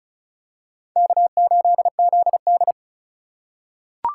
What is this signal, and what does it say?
Morse code, heard as a keyed tone, about 36 wpm, 700 Hz. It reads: K8ZD
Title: USA Call Signs - 4 Characters - 8x Character Spacing - 35wpm